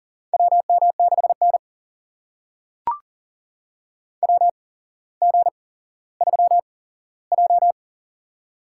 Morse code, heard as a keyed tone, about 40 wpm, spaced as for 17 wpm, 700 Hz. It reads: WM6N WG3J
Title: USA Call Signs - 8x Character Spacing - 40wpm